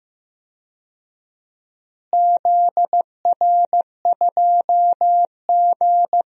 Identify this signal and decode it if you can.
Morse code, heard as a keyed tone, about 15 wpm, 700 Hz. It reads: ZR2G
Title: DXCC-100 Call Signs - 4 Characters - Repeated 3x - 15wpm